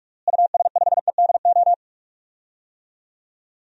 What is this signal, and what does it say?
Morse code, heard as a keyed tone, about 42 wpm, 700 Hz. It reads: US5EDO
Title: DXCC-100 Call Signs - 6 Characters - Repeated 2x - 45wpm